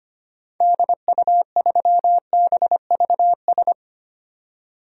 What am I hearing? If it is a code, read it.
Morse code, heard as a keyed tone, about 25 wpm, 700 Hz. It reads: DU3BVS
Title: DXCC-100 Call Signs - 6 Characters - 25wpm